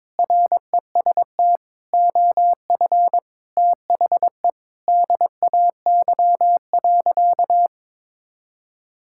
Morse code, heard as a keyed tone, about 21 wpm, 700 Hz. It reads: REST OF THE DAY.
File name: practice / 22wpm / Sentences with 2-Word Phrases - 22wpm